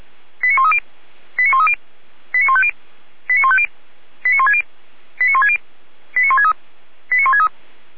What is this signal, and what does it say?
Five-tone selective call (ZVEI), heard as quick runs of five tones, tones of about 70 ms. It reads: 89130 89130 89150 89150 89160 89160 89163 89163